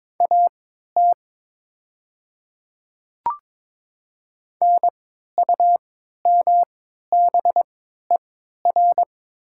Morse code, heard as a keyed tone, about 22 wpm, spaced as for 12 wpm, 700 Hz. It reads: AT NUMBER
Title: 3x Character Spacing - Top 100 Words - 22wpm